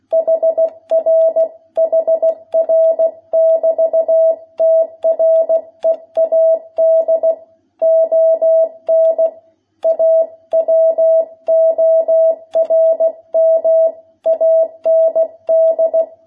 Morse code, heard as a keyed tone, about 16 wpm, 660 Hz. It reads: HR HR <BT> TREAD ON A WORM AND
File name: jh8cbh / 2021cw